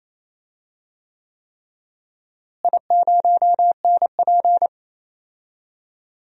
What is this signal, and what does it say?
Morse code, heard as a keyed tone, about 28 wpm, 700 Hz. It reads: I0NP
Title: DXCC-100 Call Signs - 4 Characters - Repeated 3x - 28wpm